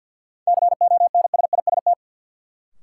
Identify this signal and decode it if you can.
Morse code, heard as a keyed tone, about 52 wpm, 700 Hz. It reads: CONSIST
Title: Sentences from Top 1400 Words - New Word Drill - 50wpm